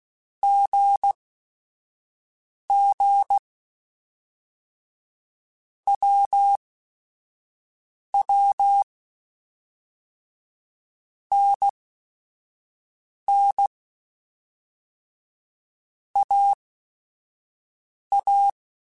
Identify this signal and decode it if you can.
Morse code, heard as a keyed tone, about 16 wpm, spaced as for 5 wpm, 800 Hz. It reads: GG WW NN AA